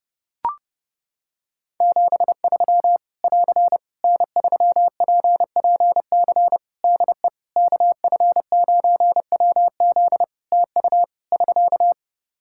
Morse code, heard as a keyed tone, about 30 wpm, 700 Hz. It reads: 73 <AR> N3PPC DE KF9WZ TU <SK>
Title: K1USN SST QSO Practice 30wpm